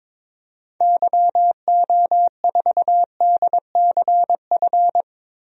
Morse code, heard as a keyed tone, about 22 wpm, 700 Hz. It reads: YO4DCF
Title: DXCC-100 Call Signs - 6 Characters - Repeated 2x - 22wpm